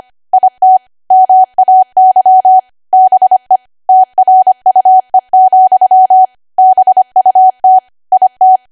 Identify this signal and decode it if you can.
Morse code, heard as a keyed tone, about 25 wpm, 750 Hz. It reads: IT MAY BE TRUE, BUT IT
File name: cwa / W1AW-Level3 / 25WPM